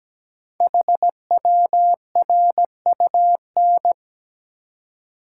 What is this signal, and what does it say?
Morse code, heard as a keyed tone, about 17 wpm, 700 Hz. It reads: HWRUN
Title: Code Groups - 5 Letters - 17wpm